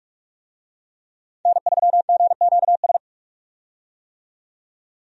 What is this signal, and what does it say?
Morse code, heard as a keyed tone, about 46 wpm, 700 Hz. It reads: N3GQS
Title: USA Call Signs - 5 Characters - 45wpm